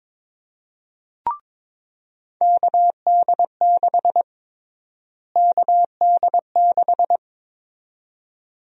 Morse code, heard as a keyed tone, about 22 wpm, 700 Hz. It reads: KD6 KD6
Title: USA Call Sign Prefixes - Repeated 2x - 22wpm